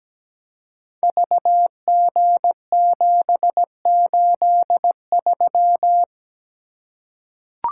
Morse code, heard as a keyed tone, about 17 wpm, 700 Hz. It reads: VG783
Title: Code Groups - 5 Letter:Numbers - 17wpm